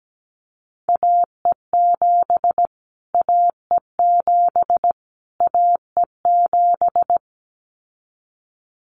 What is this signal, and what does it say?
Morse code, heard as a keyed tone, about 17 wpm, 700 Hz. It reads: AE7 AE7 AE7
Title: Callsign Prefixes for Top 25 DXCC - Repeated 3x - 17wpm